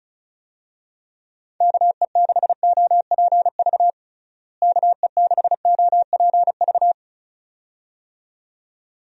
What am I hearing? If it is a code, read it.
Morse code, heard as a keyed tone, about 34 wpm, 700 Hz. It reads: KE6OPV KE6OPV
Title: USA Call Signs - 6 Characters - Repeated 2x - 35wpm